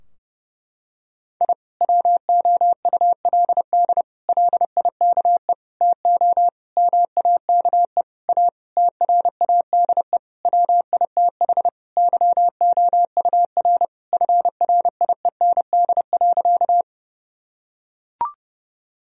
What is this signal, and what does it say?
Morse code, heard as a keyed tone, about 30 wpm, 700 Hz. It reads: I WOULD LIKE TO MAKE A TRADE WITH YOUR FRIEND.
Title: Sentences from Top 1000 Words - Review Part 4 - 30wpm